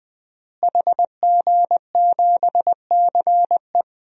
Morse code, heard as a keyed tone, about 21 wpm, 700 Hz. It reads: HG7CE
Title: DXCC-100 Call Signs - 5 Characters - Repeated 3x - 20wpm